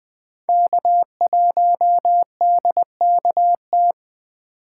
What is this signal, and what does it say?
Morse code, heard as a keyed tone, about 20 wpm, 700 Hz. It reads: K1DKT
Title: Speed Racing - USA Call Signs 17wpm